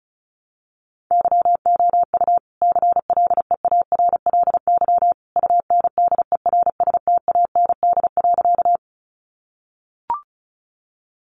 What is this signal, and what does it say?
Morse code, heard as a keyed tone, about 34 wpm, 700 Hz. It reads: YOU CLEARLY UNDERSTAND.
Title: Sentences from Top 1300 Words 35wpm